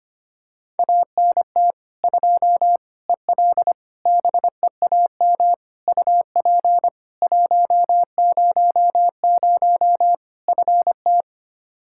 Morse code, heard as a keyed tone, about 25 wpm, 700 Hz. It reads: ANT 2 EL BEAM UP 100 FT
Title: K1USN SST QSO Practice 25wpm